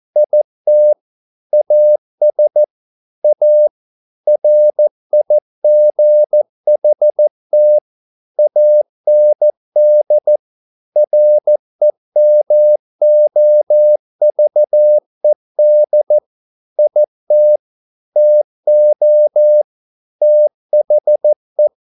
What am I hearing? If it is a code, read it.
Morse code, heard as a keyed tone, about 14 wpm, 600 Hz. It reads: IT AS A RIGHT AND REMOVED IT TO THE
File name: mp3 files / a dogs mistake - 14wpm